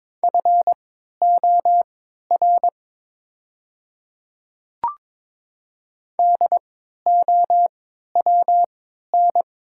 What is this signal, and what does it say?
Morse code, heard as a keyed tone, about 23 wpm, spaced as for 13 wpm, 690 Hz. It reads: FOR DOWN
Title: 3x Character Spacing - Top 500 Words - 22wpm